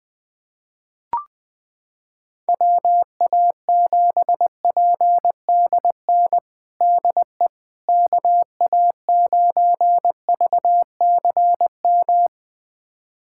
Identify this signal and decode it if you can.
Morse code, heard as a keyed tone, about 20 wpm, 700 Hz. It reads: WA7PDN DE KA9VCM
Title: USA Callsign DE Callsign 20wpm